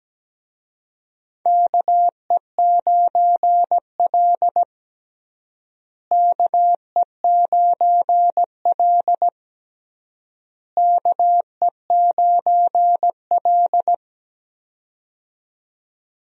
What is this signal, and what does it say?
Morse code, heard as a keyed tone, about 17 wpm, 700 Hz. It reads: KE9L KE9L KE9L
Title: USA Call Signs - 4 Characters - Repeated 3x - 17wpm